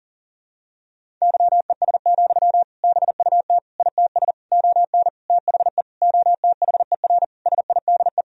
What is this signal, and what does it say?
Morse code, heard as a keyed tone, about 40 wpm, 700 Hz. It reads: YES, BUT ITS ON THE OTHER SIDE
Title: Sentences from Top 1000 Words - Review Part 1 - 40wpm